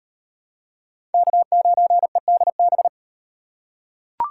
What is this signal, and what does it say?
Morse code, heard as a keyed tone, about 38 wpm, 700 Hz. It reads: K9EDB